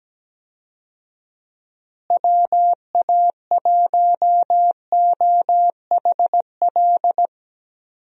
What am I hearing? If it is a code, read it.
Morse code, heard as a keyed tone, about 17 wpm, 700 Hz. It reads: WA1OHL
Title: USA Call Signs - 6 Characters - Repeated 2x - 17wpm